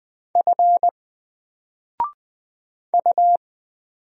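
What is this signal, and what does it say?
Morse code, heard as a keyed tone, about 20 wpm, 700 Hz. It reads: F U
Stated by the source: Single letter and number Morse drill at 20 wpm, from N4LCD.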